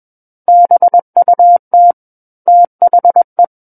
Morse code, heard as a keyed tone, about 21 wpm, 700 Hz. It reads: BUT THE
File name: Book One / 12-Chapter 12 - 21 WPM